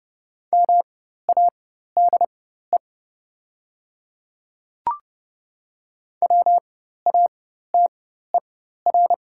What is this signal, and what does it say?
Morse code, heard as a keyed tone, about 30 wpm, spaced as for 14 wpm, 700 Hz. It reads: MADE WATER
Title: 4x Character Spacing - Top 200 Words - 30wpm